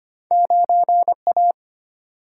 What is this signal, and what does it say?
Morse code, heard as a keyed tone, about 25 wpm, 700 Hz. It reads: 9A
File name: Two Letter-Number Combinations - 25wpm